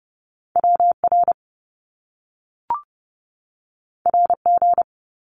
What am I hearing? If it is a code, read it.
Morse code, heard as a keyed tone, about 30 wpm, 700 Hz. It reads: WR RG
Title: Top Two Letter Combinations in Words 30wpm